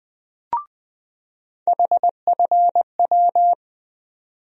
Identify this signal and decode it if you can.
Morse code, heard as a keyed tone, about 20 wpm, 700 Hz. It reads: HFW
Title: USA Call Sign Postfixes - 3 Letters 20wpm